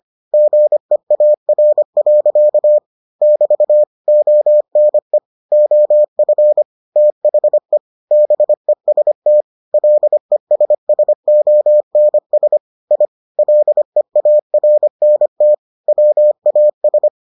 Morse code, heard as a keyed tone, about 25 wpm, 600 Hz. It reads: GEAR. <BT> ONE OF THE BEST LESSONS I LEARNT WAS